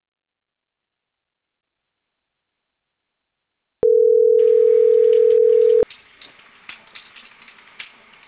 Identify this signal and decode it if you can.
Ringback tone (North America), 440 + 480 Hz.